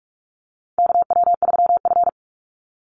Morse code, heard as a keyed tone, about 44 wpm, 700 Hz. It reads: KW3F
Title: Call Signs - 45wpm